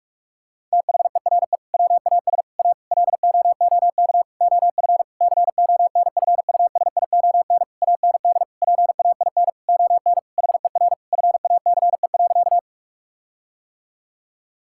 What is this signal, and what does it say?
Morse code, heard as a keyed tone, about 46 wpm, 700 Hz. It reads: THERE WAS A LOOK OF CONFUSION AND PAIN ON HER FACE.
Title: Sentences from Top 1600 Words 45wpm